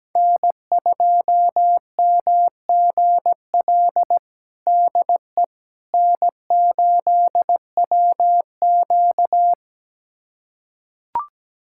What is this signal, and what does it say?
Morse code, heard as a keyed tone, about 17 wpm, 700 Hz. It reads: N2MGL DE N8WQ
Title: USA Callsign DE Callsign 17wpm